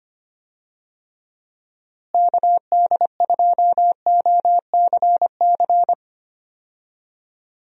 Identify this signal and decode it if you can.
Morse code, heard as a keyed tone, about 25 wpm, 700 Hz. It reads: KD2OCC
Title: USA Call Signs - 6 Characters - Repeated 2x - 25wpm